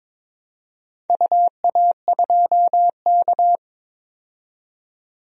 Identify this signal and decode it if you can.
Morse code, heard as a keyed tone, about 22 wpm, 700 Hz. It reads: UA2K
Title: DXCC-100 Call Signs - 4 Characters - 22wpm